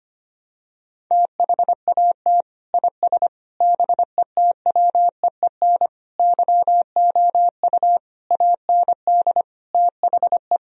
Morse code, heard as a keyed tone, about 25 wpm, 700 Hz. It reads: THAT IS BETWEEN YOU AND THE
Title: Sentences from Top 1000 Words - Review Part 4 - 25wpm